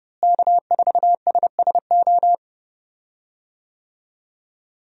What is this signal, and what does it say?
Morse code, heard as a keyed tone, about 30 wpm, 700 Hz. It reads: K4SSO
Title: USA Call Signs - 5 Characters - Repeated 2x - 30wpm